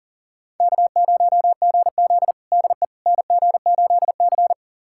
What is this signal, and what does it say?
Morse code, heard as a keyed tone, about 40 wpm, 700 Hz. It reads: K0GZ DE NG8C
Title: USA Callsign DE Callsign - 4 Characters - 40wpm